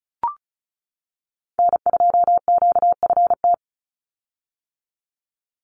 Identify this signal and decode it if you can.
Morse code, heard as a keyed tone, about 34 wpm, 700 Hz. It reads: N2QFT
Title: USA Call Signs 35wpm